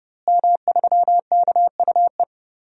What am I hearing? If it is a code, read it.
Morse code, heard as a keyed tone, about 30 wpm, 700 Hz. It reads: M3KUE
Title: DXCC-100 Call Signs - 5 Characters - Repeated 2x - 30wpm